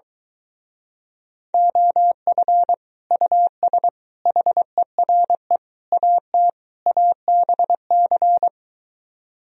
Morse code, heard as a keyed tone, about 23 wpm, 700 Hz. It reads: OF US HERE AT ABC